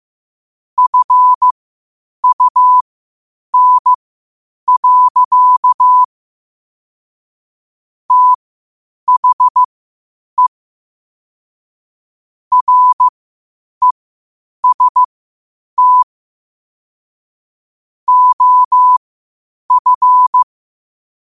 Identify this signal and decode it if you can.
Morse code, heard as a keyed tone, about 15 wpm, spaced as for 8 wpm, 1000 Hz. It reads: FUN. THE REST OF
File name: qotdfiles / 8wpm